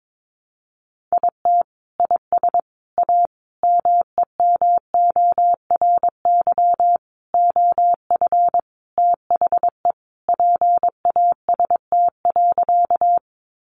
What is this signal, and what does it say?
Morse code, heard as a keyed tone, about 22 wpm, 700 Hz. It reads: IT IS A MEMORY OF THE PAST.
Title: Sentences from Top 1300 Words 22wpm